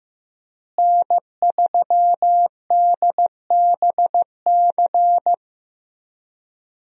Morse code, heard as a keyed tone, about 15 wpm, 700 Hz. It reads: N3DBC
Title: USA Call Signs - 5 Characters - Repeated 2x - 15wpm